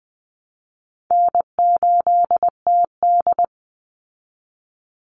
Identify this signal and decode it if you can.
Morse code, heard as a keyed tone, about 20 wpm, 700 Hz. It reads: N8TD
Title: Call Signs Repeated 3x - 20wpm